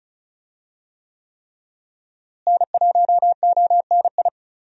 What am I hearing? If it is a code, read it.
Morse code, heard as a keyed tone, about 36 wpm, 700 Hz. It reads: N1ONI
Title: USA Call Signs - 5 Characters - Repeated 3x - 35wpm